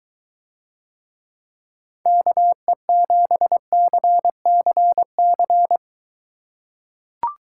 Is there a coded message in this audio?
Yes: Morse code from 2.1 to 5.8 s, heard as a keyed tone, about 23 wpm, 700 Hz. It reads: KE7CCC